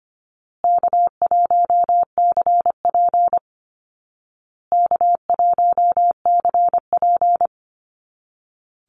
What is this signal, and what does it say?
Morse code, heard as a keyed tone, about 25 wpm, 700 Hz. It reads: K1CP K1CP
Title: Easy Call Signs Repeated 3x - 25wpm